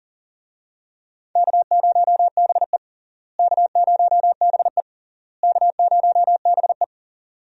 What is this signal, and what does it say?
Morse code, heard as a keyed tone, about 40 wpm, 700 Hz. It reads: K0BE K0BE K0BE
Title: USA Call Signs - 4 Characters - Repeated 3x - 40wpm